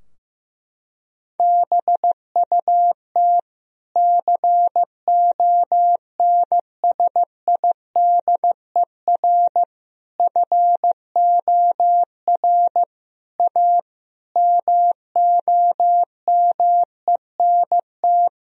Morse code, heard as a keyed tone, about 15 wpm, 700 Hz. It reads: BUT CONSIDER FOR A MOMENT
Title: Sentences from Top 1000 Words - Review Part 4 - 15wpm